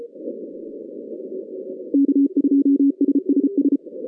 Morse code, heard as a keyed tone, about 33 wpm, 280 Hz. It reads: K2SSS
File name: 2015 cq ww cw / 2015 cqwww k5tr - 0-18